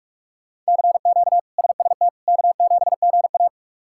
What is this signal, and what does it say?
Morse code, heard as a keyed tone, about 46 wpm, 700 Hz. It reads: CQ SST K7GA
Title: K1USN SST QSO Practice - 45wpm